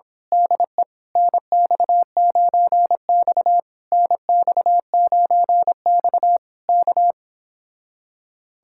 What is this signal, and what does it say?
Morse code, heard as a keyed tone, about 26 wpm, 700 Hz. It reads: DE NX9X NX9X K
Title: Speed-Racing Common QSO Phrases Encore - 22wpm